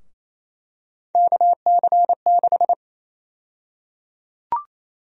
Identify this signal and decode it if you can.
Morse code, heard as a keyed tone, about 28 wpm, 700 Hz. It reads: KC6
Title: USA Call Sign Prefixes - 28wpm